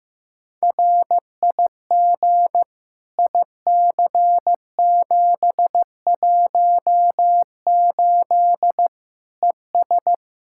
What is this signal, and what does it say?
Morse code, heard as a keyed tone, about 15 wpm, 700 Hz. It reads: RIG IC718 ES
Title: Practice QSO - Exchange 2 - Info - 15wpm